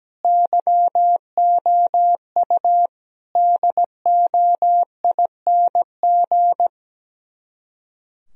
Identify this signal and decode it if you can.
Morse code, heard as a keyed tone, about 17 wpm, 700 Hz. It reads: YOU DOING
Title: Speed-Racing - Sets of 4 Words 17wpm